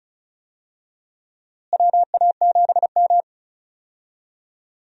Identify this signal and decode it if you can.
Morse code, heard as a keyed tone, about 36 wpm, 700 Hz. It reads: WA7M
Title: USA Call Signs - 4 Characters - Repeated 2x - 35wpm